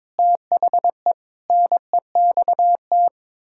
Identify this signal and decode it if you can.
Morse code, heard as a keyed tone, about 22 wpm, 700 Hz. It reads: THE NEXT
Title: Pangram Challenge v2 - 22wpm